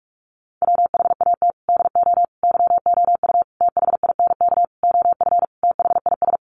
Morse code, heard as a keyed tone, about 44 wpm, 700 Hz. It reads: WHAT DO YOU THINK OF THIS
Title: Sentences from Top 1200 Words 45wpm